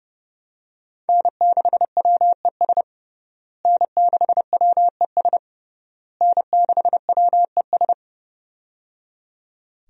Morse code, heard as a keyed tone, about 30 wpm, 700 Hz. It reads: N6WES N6WES N6WES
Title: USA Call Signs - 5 Characters - Repeated 3x - 30wpm